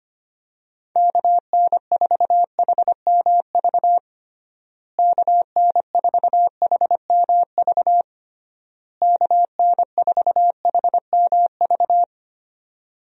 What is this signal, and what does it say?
Morse code, heard as a keyed tone, about 25 wpm, 700 Hz. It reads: KN4HMV KN4HMV KN4HMV
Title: USA Call Signs - 6 Characters - Repeated 3x - 25wpm